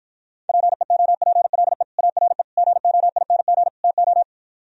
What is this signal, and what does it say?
Morse code, heard as a keyed tone, about 45 wpm, 700 Hz. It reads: PEOPLE ARE GOING TO